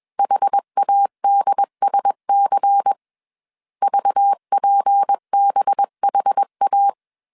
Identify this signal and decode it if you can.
Morse code, heard as a keyed tone, about 21 wpm, 800 Hz. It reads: HADSC VPBHA
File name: CW / DET / 1 6 1500C S B 20wpm 2048